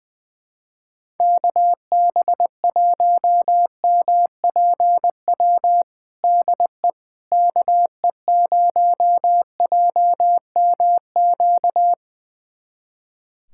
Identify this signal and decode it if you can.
Morse code, heard as a keyed tone, about 19 wpm, 700 Hz. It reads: KB1MPW DE KE0JMQ
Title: USA Callsign DE Callsign - 6 Characters - 20wpm